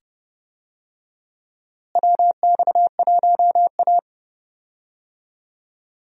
Morse code, heard as a keyed tone, about 30 wpm, 700 Hz. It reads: WX1A
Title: USA Call Signs - 4 Characters - Repeated 2x - 30wpm